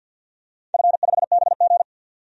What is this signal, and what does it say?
Morse code, heard as a keyed tone, about 52 wpm, 700 Hz. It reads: F5BG
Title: DXCC-100 Call Signs - 4 Characters - 50wpm